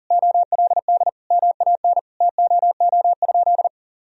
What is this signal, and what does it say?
Morse code, heard as a keyed tone, about 40 wpm, 700 Hz. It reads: OLD MAN TOO?
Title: Sentences from Top 1000 Words - Review Part 4 - 40wpm